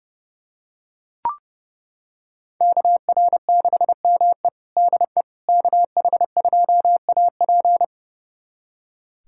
Morse code, heard as a keyed tone, about 32 wpm, 700 Hz. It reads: KR6ME DE KH2AP
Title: USA Callsign DE Callsign - 5 Characters - 30wpm